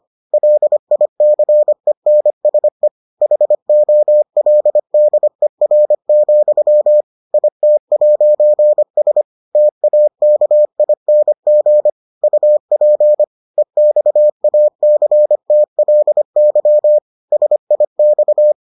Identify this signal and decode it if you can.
Morse code, heard as a keyed tone, about 25 wpm, 600 Hz. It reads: LICENSE HOLDER, IT'S TAKING UP EXACTLY SIX